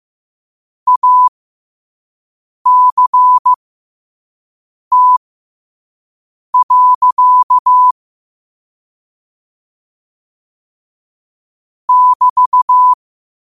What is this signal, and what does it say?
Morse code, heard as a keyed tone, about 15 wpm, spaced as for 5 wpm, 1000 Hz. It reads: ACT. <BT>
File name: qotdfiles / 5wpm